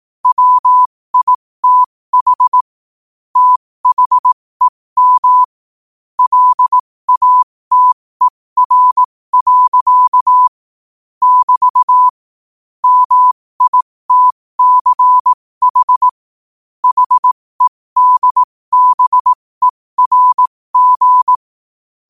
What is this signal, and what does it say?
Morse code, heard as a keyed tone, about 18 wpm, spaced as for 15 wpm, 1000 Hz. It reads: WITH THEM LATER. <BT> MITCH HEDBERG
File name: qotdfiles / 15wpm